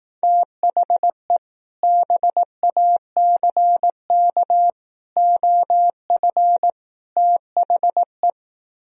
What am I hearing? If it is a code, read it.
Morse code, heard as a keyed tone, about 18 wpm, 700 Hz. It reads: THE BACK OF THE
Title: Speed-Racing - Sets of 4 Words 15wpm